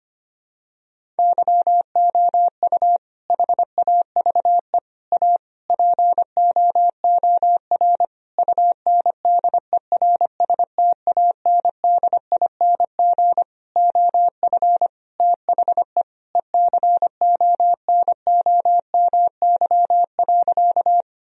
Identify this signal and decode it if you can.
Morse code, heard as a keyed tone, about 25 wpm, 700 Hz. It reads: YOU HAVE A POOR UNDERSTANDING OF THE ECONOMY.
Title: Sentences from Top 1000 Words - Review Part 3 - 25wpm